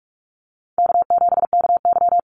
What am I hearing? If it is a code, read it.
Morse code, heard as a keyed tone, about 44 wpm, 700 Hz. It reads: K7KY
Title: Easy Call Signs Repeated 2x - 45wpm